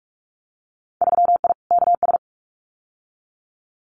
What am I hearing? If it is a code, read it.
Morse code, heard as a keyed tone, about 44 wpm, 700 Hz. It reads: 3I KS
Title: Winter Field Day Exchanges 45wpm